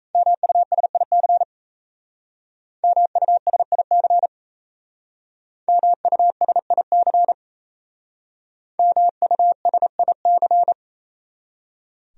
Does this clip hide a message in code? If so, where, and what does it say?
Yes: Morse code from 0.1 to 10.7 s, heard as a keyed tone, about 38 wpm, 700 Hz. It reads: MUSIC MUSIC MUSIC MUSIC